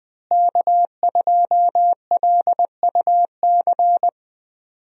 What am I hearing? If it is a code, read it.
Morse code, heard as a keyed tone, about 20 wpm, 700 Hz. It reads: K2LUC
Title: USA Call Signs - 5 Characters - Repeated 3x - 20wpm